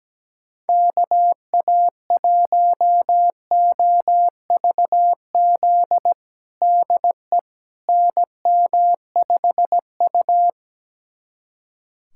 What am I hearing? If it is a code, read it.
Morse code, heard as a keyed tone, about 17 wpm, 700 Hz. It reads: KA1OVZ DE NM5U
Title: USA Callsign DE Callsign 17wpm